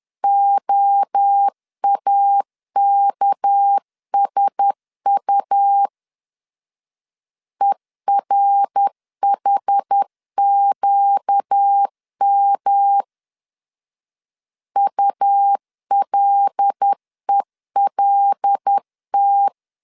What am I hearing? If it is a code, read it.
Morse code, heard as a keyed tone, about 10 wpm, 800 Hz. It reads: OAKSU ERHQM ULELT
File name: CW / DET / 1 6 1500C S A 10wpm 4141